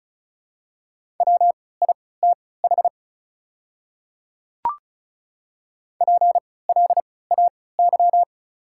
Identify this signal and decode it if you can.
Morse code, heard as a keyed tone, about 36 wpm, spaced as for 20 wpm, 700 Hz. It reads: WITH PLAY
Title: 3x Character Spacing - Top 300 Words - 35wpm